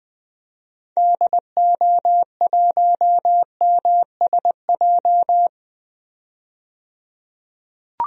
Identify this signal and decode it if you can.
Morse code, heard as a keyed tone, about 20 wpm, 700 Hz. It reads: DO1MSJ
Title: DXCC-100 Call Signs - 6 Characters - 20wpm